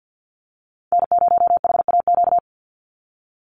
Morse code, heard as a keyed tone, about 50 wpm, 700 Hz. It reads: N0HAQ
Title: USA Call Signs 50wpm